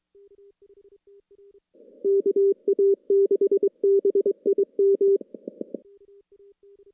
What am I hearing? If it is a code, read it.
Morse code, heard as a keyed tone, about 23 wpm, 400 Hz. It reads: KA6BIM
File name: contests / CW-SS-k5tr- -2300